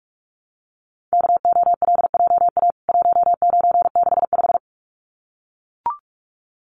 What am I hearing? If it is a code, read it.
Morse code, heard as a keyed tone, about 44 wpm, 700 Hz. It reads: KOLJA 1965
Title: CWT Exchange 45wpm